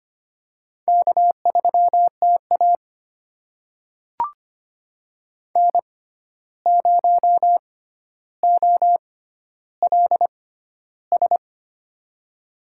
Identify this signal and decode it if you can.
Morse code, heard as a keyed tone, about 25 wpm, spaced as for 12 wpm, 700 Hz. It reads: K3TA N0OLS
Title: USA Call Signs - 6x Character Spacing - 25wpm